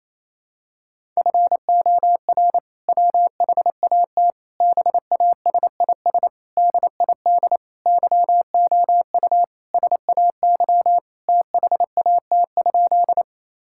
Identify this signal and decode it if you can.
Morse code, heard as a keyed tone, about 28 wpm, 700 Hz. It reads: FOR WHAT BASIS DID YOU SAY THAT?
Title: Sentences from Top 1000 Words - Review Part 5 - 28wpm